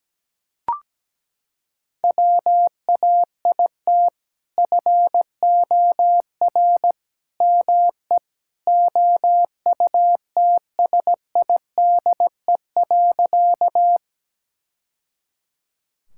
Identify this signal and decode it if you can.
Morse code, heard as a keyed tone, about 17 wpm, 700 Hz. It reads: WAIT FOR ME OUTSIDE.
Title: Sentences from Top 1000 Words - Review Part 6 - 17wpm